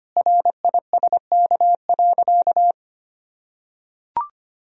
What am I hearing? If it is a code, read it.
Morse code, heard as a keyed tone, about 25 wpm, 700 Hz. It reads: RISK.
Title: Sentences from Top 1000 Words - Review Part 4 - 25wpm